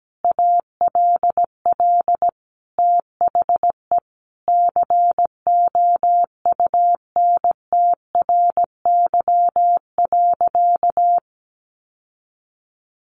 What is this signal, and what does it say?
Morse code, heard as a keyed tone, about 17 wpm, 700 Hz. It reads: ALL THE COUNTRY.
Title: Sentences from Top 1200 Words 17wpm